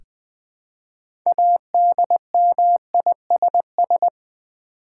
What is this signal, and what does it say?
Morse code, heard as a keyed tone, about 20 wpm, 700 Hz. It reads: ADMISS
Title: Build A Word - 3 Letters At A Time - 9-Letter Words - 20wpm